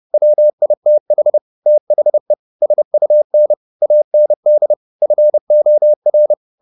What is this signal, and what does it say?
Morse code, heard as a keyed tone, about 30 wpm, 600 Hz. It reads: WITH THE SUN AND FOR
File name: the old man and the sea / the old man and the sea 1 30wpm